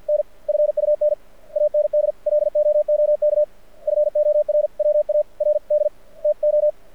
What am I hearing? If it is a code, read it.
Morse code, heard as a keyed tone, about 54 wpm, spaced as for 46 wpm, 600 Hz. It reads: N FUN AND LOOK FORWARD TO
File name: archived / CRSnet